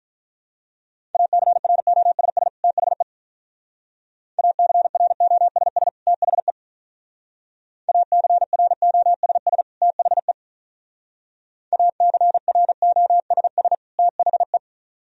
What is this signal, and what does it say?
Morse code, heard as a keyed tone, about 43 wpm, 700 Hz. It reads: ACROSS THE ACROSS THE ACROSS THE ACROSS THE
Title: Speed-Racing - Sets of 2 Words 35wpm